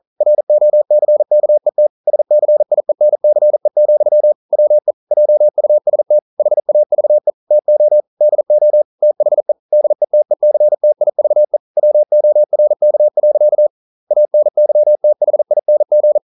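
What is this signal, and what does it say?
Morse code, heard as a keyed tone, about 42 wpm, 600 Hz. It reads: ROCKET SCIENCE, WE JUST HAVE TO DO THE DETECTIVE WORK. ANYTHING
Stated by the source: Morse code practice at 41 wpm, from CW med Gnister.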